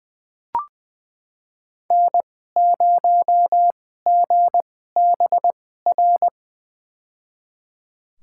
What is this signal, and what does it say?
Morse code, heard as a keyed tone, about 20 wpm, spaced as for 15 wpm, 700 Hz. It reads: N0GBR
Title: USA Call Signs - 5 Characters - 2x Character Spacing - 20wpm